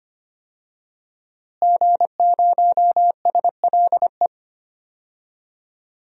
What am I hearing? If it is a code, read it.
Morse code, heard as a keyed tone, about 25 wpm, 700 Hz. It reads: G0SLE
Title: DXCC-100 Call Signs - 5 Characters - Repeated 2x - 25wpm